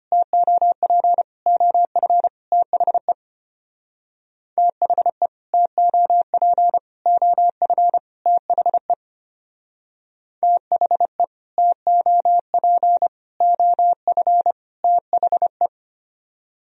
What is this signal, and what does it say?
Morse code, heard as a keyed tone, about 31 wpm, 700 Hz. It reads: TOP OF THE THE TOP OF THE THE TOP OF THE
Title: Speed-Racing - Sets of 4 Words 25wpm